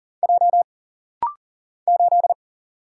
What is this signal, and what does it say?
Morse code, heard as a keyed tone, about 40 wpm, 700 Hz. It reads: J 8
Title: Single Letter-Number Alt - Rapid-Fire 40wpm